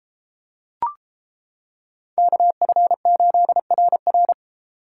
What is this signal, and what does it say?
Morse code, heard as a keyed tone, about 27 wpm, 700 Hz. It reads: KF8RR
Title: Speed Racing - USA 2x2 Call Signs 22wpm